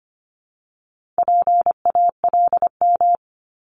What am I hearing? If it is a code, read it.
Morse code, heard as a keyed tone, about 25 wpm, 700 Hz. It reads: PALM
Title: Multi-letter Phonemes 25wpm